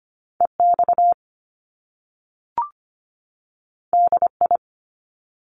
Morse code, heard as a keyed tone, about 25 wpm, 700 Hz. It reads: EX DI
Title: Top Two Letter Combinations in Words 25wpm